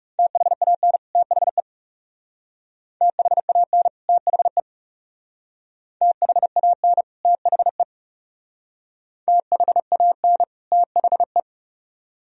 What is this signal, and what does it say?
Morse code, heard as a keyed tone, about 36 wpm, 700 Hz. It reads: THAN THE THAN THE THAN THE THAN THE
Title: Speed-Racing - Sets of 2 Words 30wpm